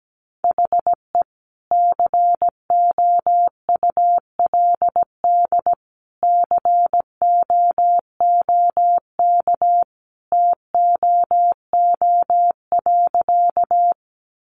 Morse code, heard as a keyed tone, about 17 wpm, 700 Hz. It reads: HE COULD COOK TOO.
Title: Sentences from Top 700 Words 17wpm